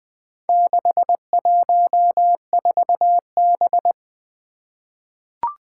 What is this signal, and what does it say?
Morse code, heard as a keyed tone, about 20 wpm, 700 Hz. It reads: 614B
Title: Four Challenging Letter-Number Combinations 20wpm